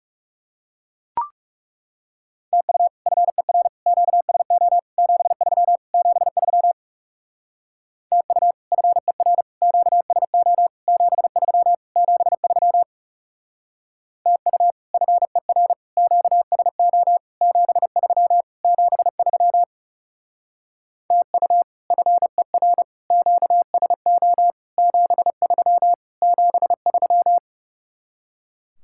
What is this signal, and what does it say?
Morse code, heard as a keyed tone, about 38 wpm, 700 Hz. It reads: TU FER QSO 73 73 TU FER QSO 73 73 TU FER QSO 73 73 TU FER QSO 73 73